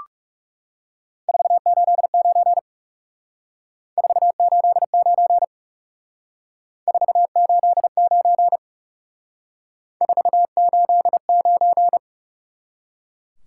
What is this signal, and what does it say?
Morse code, heard as a keyed tone, about 36 wpm, 700 Hz. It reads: 489 489 489 489
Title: Speed-Racing Top 180 QSO Elements 30wpm